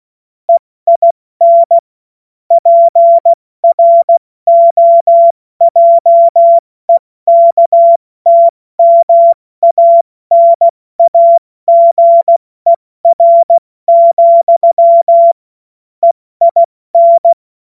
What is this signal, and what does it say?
Morse code, heard as a keyed tone, about 16 wpm, 680 Hz. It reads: EIN PROJEKTMANAGER, EIN
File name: Morsekurs / Witze / Witz 14. Dezember 2021, 70 F80